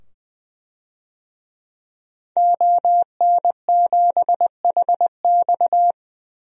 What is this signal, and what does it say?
Morse code, heard as a keyed tone, about 20 wpm, 700 Hz. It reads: ON7HX
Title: DXCC-100 Call Signs - 5 Characters - 20wpm